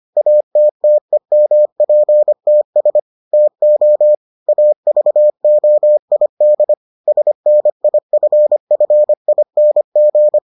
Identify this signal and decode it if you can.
Morse code, heard as a keyed tone, about 25 wpm, 600 Hz. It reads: ATTEMPTS TO AVOID SNIFFING